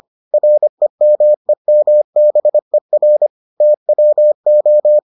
Morse code, heard as a keyed tone, about 25 wpm, 600 Hz. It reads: REMEMBER TWO